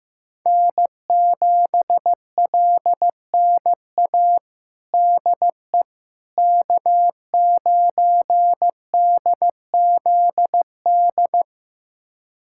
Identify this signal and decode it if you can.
Morse code, heard as a keyed tone, about 15 wpm, 700 Hz. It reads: N7LNA DE K9DZD
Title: USA Callsign DE Callsign - 5 Characters - 15wpm